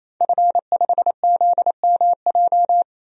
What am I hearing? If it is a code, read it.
Morse code, heard as a keyed tone, about 28 wpm, 690 Hz. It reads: F5ZMJ
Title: DXCC-100 Call Signs - 5 Characters - 28wpm